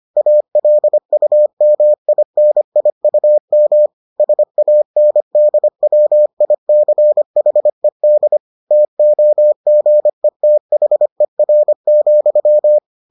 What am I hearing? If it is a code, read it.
Morse code, heard as a keyed tone, about 25 wpm, 600 Hz. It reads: ALUMINIUM SANDWICHED TOGETHER,